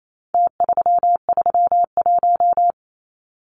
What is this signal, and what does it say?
Morse code, heard as a keyed tone, about 28 wpm, 700 Hz. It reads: T331
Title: Numbers - Contest Style - 4-Digit 28wpm